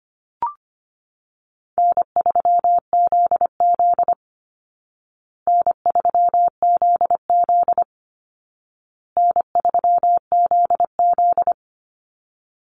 Morse code, heard as a keyed tone, about 25 wpm, 700 Hz. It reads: N3ZZ N3ZZ N3ZZ
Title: Call Signs Repeated 2x - 25wpm